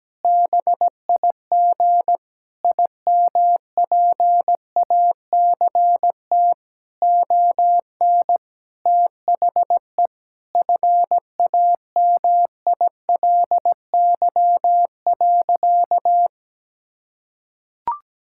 Morse code, heard as a keyed tone, about 17 wpm, 700 Hz. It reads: BIG IMPACT ON THE FAMILY.
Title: Sentences from Top 1000 Words - Review Part 6 - 17wpm